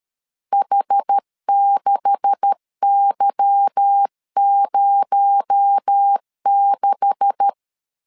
Morse code, heard as a keyed tone, about 13 wpm, 800 Hz. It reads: H6Y06
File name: LEZIONI / 9 286 12wpm 2305